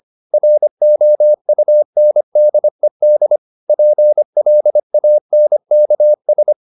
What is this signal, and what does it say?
Morse code, heard as a keyed tone, about 25 wpm, 600 Hz. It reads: ROUNDED PLANKS
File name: the old man and the sea part 1 25wpm